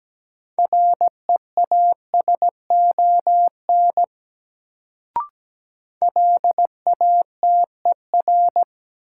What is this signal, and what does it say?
Morse code, heard as a keyed tone, about 17 wpm, 700 Hz. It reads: REASON LATER
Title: Sentences from Top 700 Words - New Word Drill - 17wpm